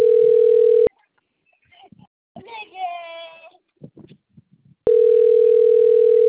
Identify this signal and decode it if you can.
Ringback tone (North America), 440 + 480 Hz.